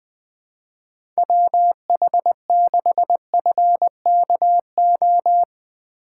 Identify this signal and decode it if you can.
Morse code, heard as a keyed tone, about 20 wpm, 700 Hz. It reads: WH6FKO
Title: USA Call Signs - 6 Characters - 4x Character Spacing - 20wpm